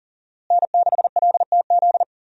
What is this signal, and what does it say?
Morse code, heard as a keyed tone, about 40 wpm, 690 Hz. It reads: N6LTZ
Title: USA Call Signs - 5 Characters - Repeated 3x - 40wpm